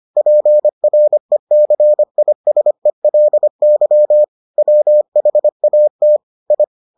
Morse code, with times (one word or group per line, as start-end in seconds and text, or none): 0.2-4.2 PRECISELY
4.6-6.2 WHAT
6.5-6.6 I